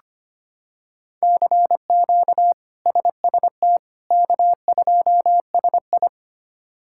Morse code, heard as a keyed tone, about 25 wpm, 700 Hz. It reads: CQ SST K2SI
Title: K1USN SST QSO Practice - 25wpm